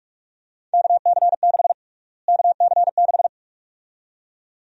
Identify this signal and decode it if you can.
Morse code, heard as a keyed tone, about 42 wpm, 700 Hz. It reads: KC6 KC6
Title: USA Call Sign Prefixes - Repeated 2x - 45wpm